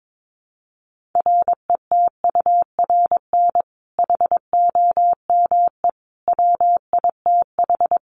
Morse code, heard as a keyed tone, about 22 wpm, 700 Hz. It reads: RETURN HOME WITH
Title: Sentences from Top 800 Words 22wpm